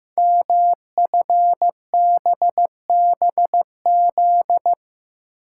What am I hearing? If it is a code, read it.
Morse code, heard as a keyed tone, about 15 wpm, 700 Hz. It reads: MFBBZ
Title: Code Groups - 5 Letters - 15wpm